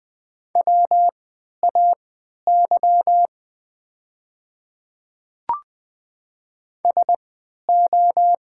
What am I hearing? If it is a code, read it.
Morse code, heard as a keyed tone, about 20 wpm, spaced as for 11 wpm, 700 Hz. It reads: WAY SO
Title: 3x Character Spacing - Top 200 Words - 20wpm